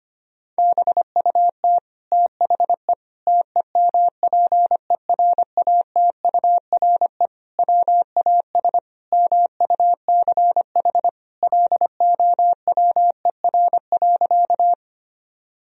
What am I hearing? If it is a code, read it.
Morse code, heard as a keyed tone, about 25 wpm, 700 Hz. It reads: BUT THE TEMPERATURE WAS MUCH LOWER.
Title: Sentences from Top 1000 Words - Review Part 4 - 25wpm